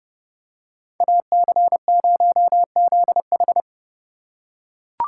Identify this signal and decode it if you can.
Morse code, heard as a keyed tone, about 31 wpm, 700 Hz. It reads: AC0ZH